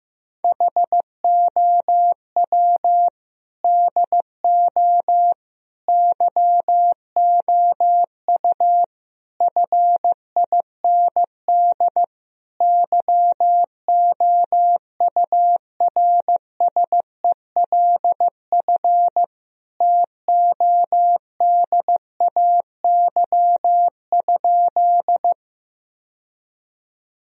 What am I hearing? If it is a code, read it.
Morse code, heard as a keyed tone, about 15 wpm, 700 Hz. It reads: HOW DO YOU FIND YOURSELF TODAY?
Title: Sentences from Top 1000 Words - Review Part 4 - 15wpm